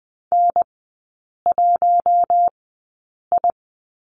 Morse code, heard as a keyed tone, about 20 wpm, spaced as for 8 wpm, 700 Hz. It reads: N1I
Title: 041 Lesson 8e - Number 1 - Three Characters 20wpm